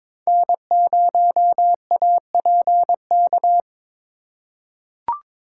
Morse code, heard as a keyed tone, about 22 wpm, 700 Hz. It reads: N0APK
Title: USA Call Signs - 5 Characters - 2x Character Spacing - 22wpm